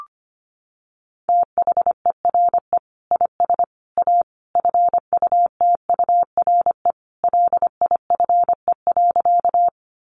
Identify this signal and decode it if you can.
Morse code, heard as a keyed tone, about 25 wpm, 700 Hz. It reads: THERE IS A FUTURE LIFE.
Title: Sentences from Top 700 Words 25wpm